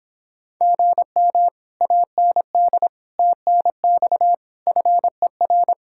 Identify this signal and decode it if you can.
Morse code, heard as a keyed tone, about 26 wpm, 700 Hz. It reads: GM AND TNX FER